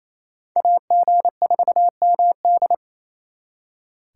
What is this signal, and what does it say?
Morse code, heard as a keyed tone, about 36 wpm, 700 Hz. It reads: AG4MD